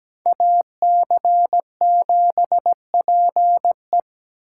Morse code, heard as a keyed tone, about 20 wpm, 700 Hz. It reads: AC7PE
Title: Speed Racing - USA 2x2 Call Signs 17wpm